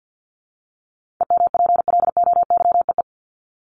Morse code, heard as a keyed tone, about 48 wpm, 700 Hz. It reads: EMPLOYEE
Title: Top 1501-1750 Words 50wpm